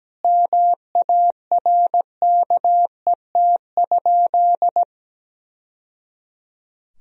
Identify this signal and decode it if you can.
Morse code, heard as a keyed tone, about 17 wpm, 700 Hz. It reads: MARKET?
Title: Sentences from Top 1000 Words - Review Part 3 - 17wpm